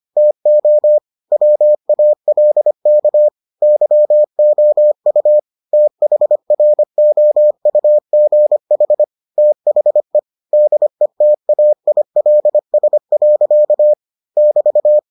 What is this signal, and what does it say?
Morse code, heard as a keyed tone, about 25 wpm, 600 Hz. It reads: TO WALK YOU THROUGH THE DETAILS. <BT>